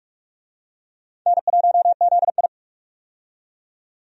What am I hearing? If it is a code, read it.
Morse code, heard as a keyed tone, about 46 wpm, 700 Hz. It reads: N1ZI
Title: USA Call Signs - 4 Characters - 45wpm